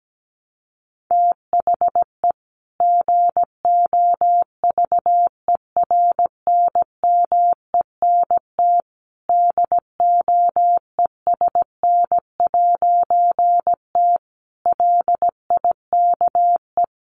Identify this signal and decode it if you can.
Morse code, heard as a keyed tone, about 17 wpm, 700 Hz. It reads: THE GOVERNMENT DOESN'T LIKE
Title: A Sound Of Thunder 17wpm